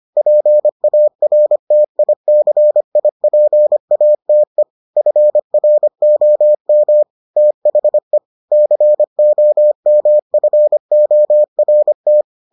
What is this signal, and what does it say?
Morse code, heard as a keyed tone, about 25 wpm, 600 Hz. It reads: PARTICIPATE FROM THE COMFORT